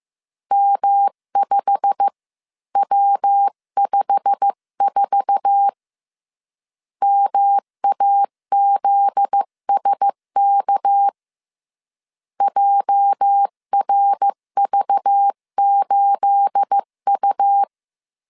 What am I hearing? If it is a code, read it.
Morse code, heard as a keyed tone, about 15 wpm, 800 Hz. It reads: M5W54 MAZSK JRV8U
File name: CW / DET / 1 9 1500C S B 14wpm 3302